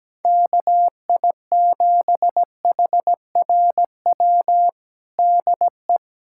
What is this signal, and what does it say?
Morse code, heard as a keyed tone, about 17 wpm, 700 Hz. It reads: KI7HRW DE
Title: USA Callsign DE Callsign - 6 Characters - 17wpm